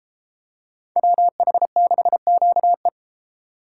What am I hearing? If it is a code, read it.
Morse code, heard as a keyed tone, about 33 wpm, 700 Hz. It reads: WH6QE